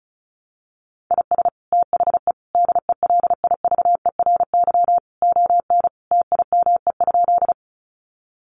Morse code, heard as a keyed tone, about 34 wpm, 700 Hz. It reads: IS THE DELIVERY ON TIME?
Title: Sentences from Top 1500 Words 35wpm